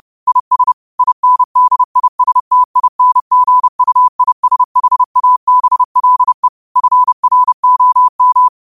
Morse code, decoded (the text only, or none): IS INDISTINGUISHABLE FROM